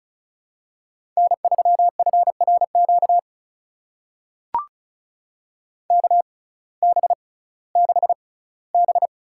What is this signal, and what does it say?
Morse code, heard as a keyed tone, about 34 wpm, spaced as for 19 wpm, 700 Hz. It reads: N3FRQ KB6B
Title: USA Call Signs - 6x Character Spacing - 35wpm